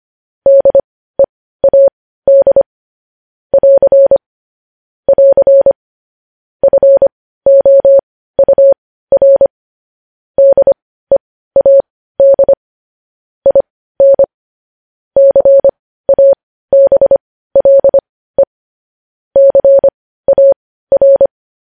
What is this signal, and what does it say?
Morse code, heard as a keyed tone, about 25 wpm, spaced as for 15 wpm, 550 Hz. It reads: DEAD <AR> <AR> FOUR DEAD IN CABLE CAR
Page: News Headlines in Morse Code at 15 WPM